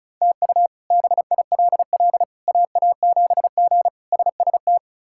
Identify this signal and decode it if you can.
Morse code, heard as a keyed tone, about 34 wpm, 700 Hz. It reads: TU BILL AA7G SST
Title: K1USN SST QSO Practice - 35wpm